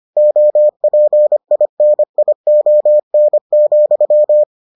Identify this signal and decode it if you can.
Morse code, heard as a keyed tone, about 25 wpm, 600 Hz. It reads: OPINION,